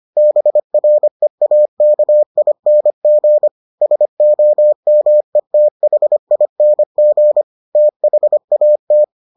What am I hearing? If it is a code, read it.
Morse code, heard as a keyed tone, about 25 wpm, 600 Hz. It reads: BREAKING SOMETHING THAT